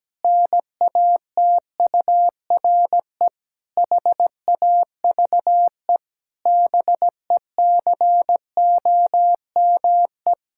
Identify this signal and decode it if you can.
Morse code, heard as a keyed tone, about 17 wpm, 700 Hz. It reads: NATURE HAVE BECOME
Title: Pangram Challenge v2 - 17wpm